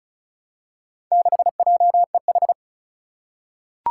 Morse code, heard as a keyed tone, about 34 wpm, 700 Hz. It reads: 6JEH
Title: Four Challenging Letter-Number Combinations 35wpm